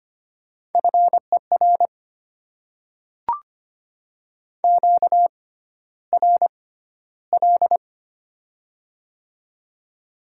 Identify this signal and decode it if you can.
Morse code, heard as a keyed tone, about 25 wpm, spaced as for 13 wpm, 700 Hz. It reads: FER QRL
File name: Top 20 QSO Elements - 6x Character Spacing - 25wpm